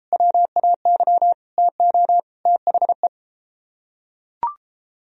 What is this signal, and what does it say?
Morse code, heard as a keyed tone, about 31 wpm, 700 Hz. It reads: WAY TO THE